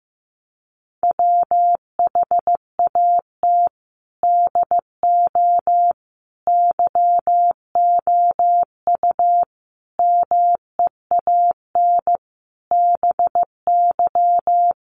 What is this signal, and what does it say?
Morse code, heard as a keyed tone, about 15 wpm, 700 Hz. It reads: WHAT DO YOU MEAN BY
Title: Sentences from Top 1200 Words 15wpm